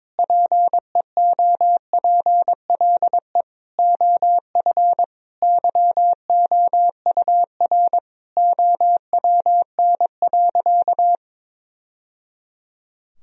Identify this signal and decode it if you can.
Morse code, heard as a keyed tone, about 22 wpm, 700 Hz. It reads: PEOPLE OF YOUR OWN.
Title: Sentences from Top 1000 Words - Review Part 2 - 22wpm